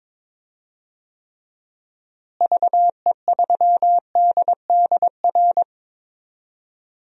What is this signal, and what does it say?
Morse code, heard as a keyed tone, about 22 wpm, 700 Hz. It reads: VE3DDR
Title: DXCC-100 Call Signs - 6 Characters - Repeated 3x - 22wpm